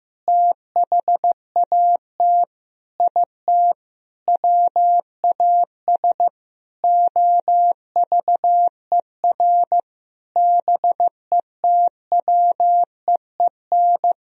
Morse code, heard as a keyed tone, about 15 wpm, 700 Hz. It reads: THAT IT WAS OVER BETWEEN
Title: Sentences from Top 1000 Words - Review Part 1 - 15wpm